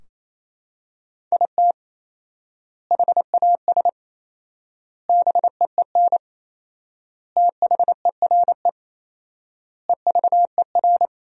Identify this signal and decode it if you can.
Morse code, heard as a keyed tone, about 27 wpm, 700 Hz. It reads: IT HAS BEEN THERE EVER
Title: 4x Word-Spacing - Sentences from Top 400 Words - 28wpm